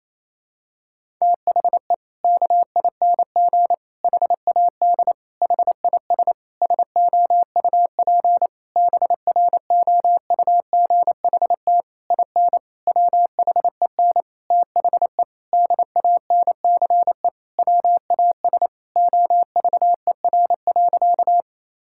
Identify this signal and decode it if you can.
Morse code, heard as a keyed tone, about 27 wpm, 700 Hz. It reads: THE KING HAD HIS SOUP BROUGHT IN WHEN THE DANCE WAS OVER.
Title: Sentences from Top 1000 Words - Review Part 6 - 28wpm